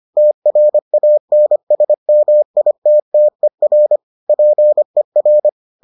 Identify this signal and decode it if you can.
Morse code, heard as a keyed tone, about 25 wpm, 600 Hz. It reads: TRANSMITTER PER